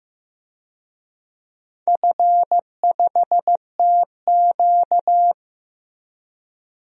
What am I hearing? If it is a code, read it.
Morse code, heard as a keyed tone, about 15 wpm, 700 Hz. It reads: F5TQ
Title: DXCC-100 Call Signs - 4 Characters - Repeated 2x - 15wpm